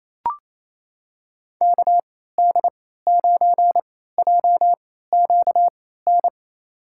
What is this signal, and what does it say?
Morse code, heard as a keyed tone, about 28 wpm, spaced as for 16 wpm, 700 Hz. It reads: KD9JQN
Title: USA Call Signs - 6 Characters - 3x Character Spacing - 28wpm